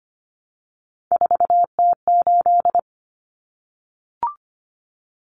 Morse code, heard as a keyed tone, about 25 wpm, 700 Hz. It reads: 4T8
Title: Common Cut Numbers - 3-Digit 25wpm